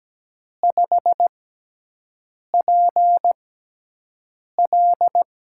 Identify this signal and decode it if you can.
Morse code, heard as a keyed tone, about 17 wpm, spaced as for 6 wpm, 700 Hz. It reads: 5PL
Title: USA Call Signs - 6x Character Spacing - 17wpm